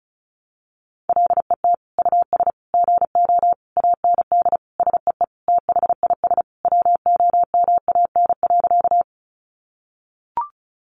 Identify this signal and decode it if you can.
Morse code, heard as a keyed tone, about 34 wpm, 700 Hz. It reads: LET US GO AND SEE THIS WOMAN.
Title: Sentences from Top 1100 Words 35wpm